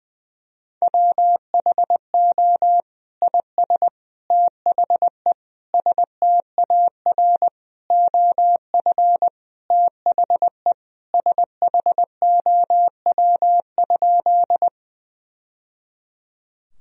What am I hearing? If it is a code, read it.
Morse code, heard as a keyed tone, about 21 wpm, 700 Hz. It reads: WHO IS THE STAR OF THE SHOW?
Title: Sentences from Top 1000 Words - Review Part 2 - 20wpm